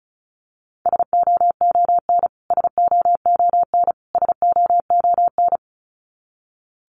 Morse code, heard as a keyed tone, about 34 wpm, 700 Hz. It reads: SOON SOON SOON
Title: Top 500 Words Repeated 35wpm